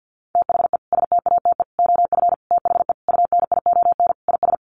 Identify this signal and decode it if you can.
Morse code, heard as a keyed tone, about 50 wpm, 700 Hz. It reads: THE STATE OF THE UNION IS
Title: Sentences from Top 1400 Words 50wpm